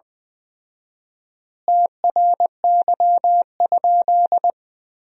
Morse code, heard as a keyed tone, about 21 wpm, 700 Hz. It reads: TRY?
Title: 4x Word-Spacing - Sentences from Top 200 Words - 20wpm